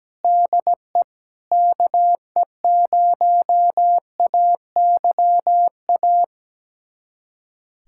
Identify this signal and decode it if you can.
Morse code, heard as a keyed tone, about 17 wpm, 700 Hz. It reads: DE KE0AYA
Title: USA Callsign DE Callsign - 6 Characters - 17wpm